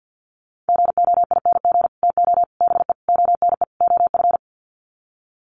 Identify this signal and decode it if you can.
Morse code, heard as a keyed tone, about 50 wpm, 700 Hz. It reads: GOING TO BE ONE OF
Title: Sets of 5 Words Encore2 50wpm